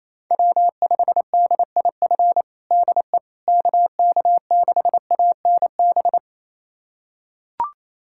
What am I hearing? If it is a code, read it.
Morse code, heard as a keyed tone, about 27 wpm, 700 Hz. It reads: W5DIF DE KK6ANB
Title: USA Callsign DE Callsign 28wpm